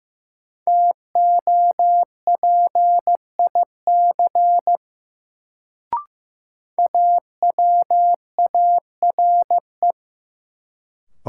Morse code, heard as a keyed tone, about 16 wpm, 700 Hz. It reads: TOPIC AWARE
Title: Sentences from Top 1200 Words - New Word Drill Alt - 15wpm